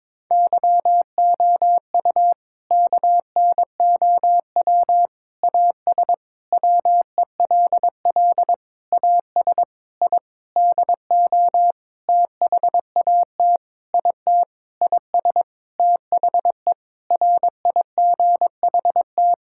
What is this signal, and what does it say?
Morse code, heard as a keyed tone, about 22 wpm, 700 Hz. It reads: YOU KNOW AS WELL AS I DO THAT IT IS THE RIGHT
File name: practice / 22wpm / Sentences with 8-Word Phrases - 22wpm